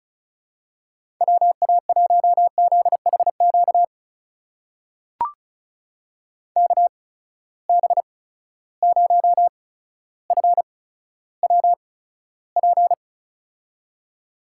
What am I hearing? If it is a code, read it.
Morse code, heard as a keyed tone, about 36 wpm, spaced as for 15 wpm, 700 Hz. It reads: WA1ZHQ KB0FWP
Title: USA Call Signs - 6 Characters - 8x Character Spacing - 35wpm